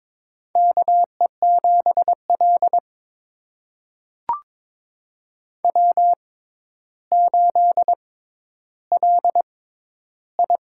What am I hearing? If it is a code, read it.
Morse code, heard as a keyed tone, about 22 wpm, spaced as for 11 wpm, 700 Hz. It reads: KE7L W8LI
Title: USA Call Signs - 4 Characters - 6x Character Spacing - 22wpm